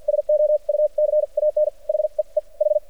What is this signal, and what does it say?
Morse code, heard as a keyed tone, about 56 wpm, spaced as for 40 wpm, 600 Hz. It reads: SO U CAN HEEH
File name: archived / CRSnet